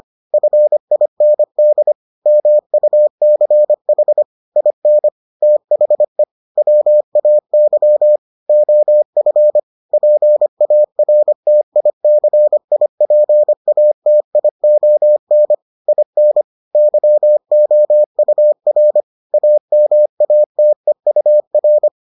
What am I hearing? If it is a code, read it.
Morse code, heard as a keyed tone, about 25 wpm, 600 Hz. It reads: FIND MUCH IN THE WAY OF PARTICIPATION IN YOUR AMATEUR